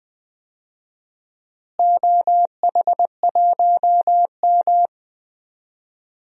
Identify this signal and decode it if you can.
Morse code, heard as a keyed tone, about 20 wpm, 700 Hz. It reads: OH1M
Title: DXCC-100 Call Signs - 4 Characters - Repeated 3x - 20wpm